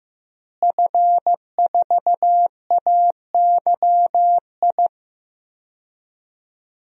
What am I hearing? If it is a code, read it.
Morse code, heard as a keyed tone, about 15 wpm, 700 Hz. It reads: F4AYI
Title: DXCC-100 Call Signs - 5 Characters - Repeated 2x - 15wpm